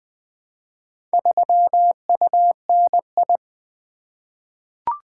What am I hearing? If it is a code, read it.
Morse code, heard as a keyed tone, about 20 wpm, 700 Hz. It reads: 3UNI
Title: Four Letter-Number Combinations - 20wpm